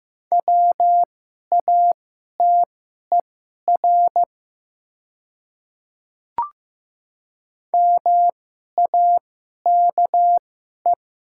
Morse code, heard as a keyed tone, about 15 wpm, spaced as for 11 wpm, 700 Hz. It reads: WATER MAKE
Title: 2x Character Spacing - Top 100 Words - 15wpm